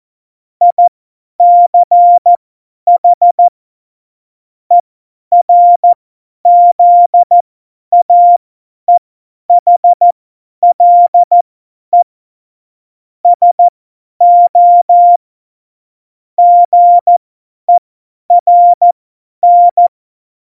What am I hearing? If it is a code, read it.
Morse code, heard as a keyed tone, about 14 wpm, spaced as for 10 wpm, 700 Hz. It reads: ICH ERZAEHLE SO GERN
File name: Morsekurs / Witze / Witz 08. Dezember 2020, 50 F70